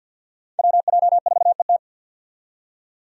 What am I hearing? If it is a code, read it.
Morse code, heard as a keyed tone, about 52 wpm, 700 Hz. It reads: WJ4ET
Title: USA Call Signs - 5 Characters - 50wpm